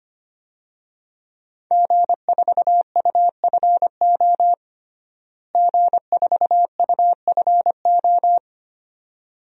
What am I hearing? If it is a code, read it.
Morse code, heard as a keyed tone, about 25 wpm, 700 Hz. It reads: G4UFO G4UFO
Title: DXCC-100 Call Signs - 5 Characters - Repeated 2x - 25wpm